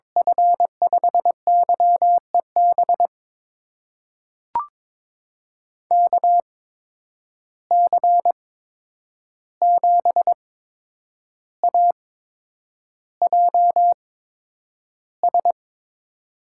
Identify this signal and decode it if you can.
Morse code, heard as a keyed tone, about 22 wpm, spaced as for 9 wpm, 700 Hz. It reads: F5YEB KC7AJS
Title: USA Call Signs - 6 Characters - 8x Character Spacing - 22wpm